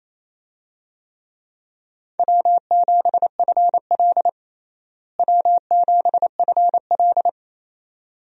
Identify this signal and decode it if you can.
Morse code, heard as a keyed tone, about 28 wpm, 700 Hz. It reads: W7FL W7FL
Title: USA Call Signs - 4 Characters - Repeated 2x - 28wpm